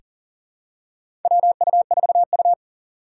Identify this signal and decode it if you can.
Morse code, heard as a keyed tone, about 40 wpm, 700 Hz. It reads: WU4U
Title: USA Call Signs - 4 Characters - 4x Character Spacing - 40wpm